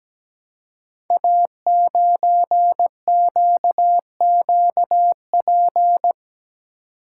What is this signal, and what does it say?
Morse code, heard as a keyed tone, about 17 wpm, 700 Hz. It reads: A9QQP
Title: Code Groups - 5 Letter:Numbers - 17wpm